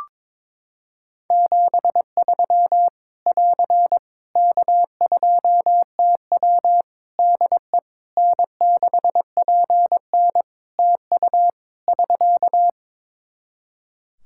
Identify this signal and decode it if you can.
Morse code, heard as a keyed tone, about 21 wpm, 700 Hz. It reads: 73 <AR> K2TW DE N6PN TU <SK>
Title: Practice QSO - Exchange 4 - Ending - 22wpm